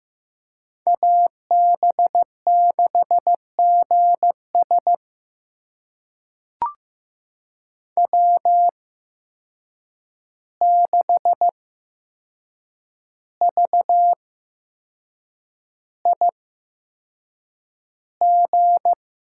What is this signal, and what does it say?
Morse code, heard as a keyed tone, about 15 wpm, spaced as for 6 wpm, 700 Hz. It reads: AB6GS W6VIG
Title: USA Call Signs - 5 Characters - 8x Character Spacing - 15wpm